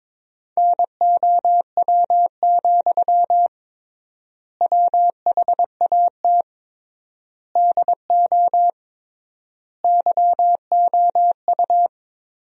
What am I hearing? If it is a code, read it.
Morse code, heard as a keyed tone, about 22 wpm, 700 Hz. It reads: NOW, WHAT DO YOU
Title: 3x Word-Spacing - Sentences from Top 200 Words - 22wpm